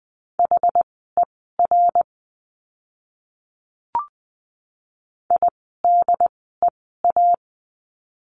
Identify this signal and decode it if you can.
Morse code, heard as a keyed tone, about 20 wpm, spaced as for 14 wpm, 700 Hz. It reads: HER IDEA
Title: 101 Lesson 17f - Letter u - Words 20wpm